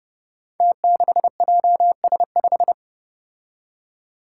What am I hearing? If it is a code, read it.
Morse code, heard as a keyed tone, about 29 wpm, 700 Hz. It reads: T6JS5
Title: Five Challenging Letter-Number Combinations 30wpm